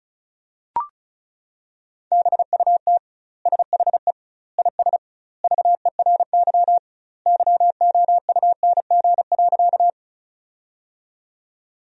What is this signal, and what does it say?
Morse code, heard as a keyed tone, about 34 wpm, 700 Hz. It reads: BUT SHE IS VERY YOUNG.
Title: 2x Word-Spacing - Sentences from Top 400 Words - 35wpm